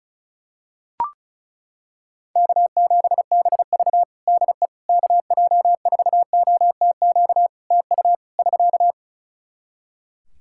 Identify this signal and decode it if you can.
Morse code, heard as a keyed tone, about 36 wpm, 700 Hz. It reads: K7BV DE KJ4OTQ TU <SK>
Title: Practice QSO - Exchange 4 - Ending - 35wpm